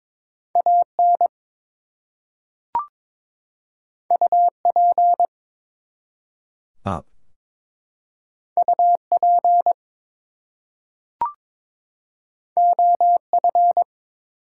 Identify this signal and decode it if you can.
Morse code, heard as a keyed tone, about 22 wpm, 700 Hz. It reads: AN UP UP OF
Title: Two Letter Words - 22wpm